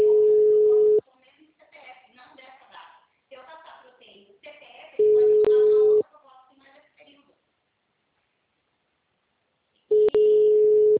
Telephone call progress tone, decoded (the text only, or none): ringback tone